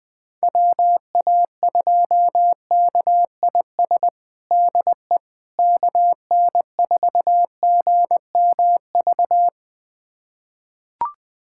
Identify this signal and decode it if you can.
Morse code, heard as a keyed tone, about 21 wpm, 700 Hz. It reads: WA2KIS DE KN4GMV
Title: USA Callsign DE Callsign 20wpm